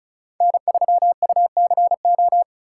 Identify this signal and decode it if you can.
Morse code, heard as a keyed tone, about 36 wpm, 700 Hz. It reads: N3UCO
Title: USA Callsign DE Callsign - 5 Characters - 35wpm